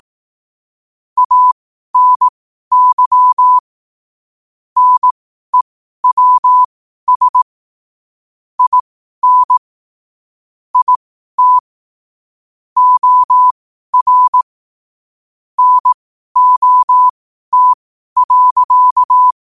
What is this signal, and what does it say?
Morse code, heard as a keyed tone, about 18 wpm, spaced as for 12 wpm, 1000 Hz. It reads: ANY NEWS IN IT OR NOT.
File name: qotdfiles / 12wpm